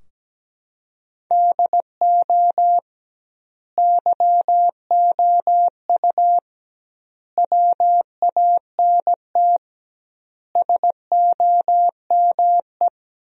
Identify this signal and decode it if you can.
Morse code, heard as a keyed tone, about 17 wpm, 700 Hz. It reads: DO YOU WANT SOME
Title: 2x Word-Spacing - Sentences from Top 300 Words - 17wpm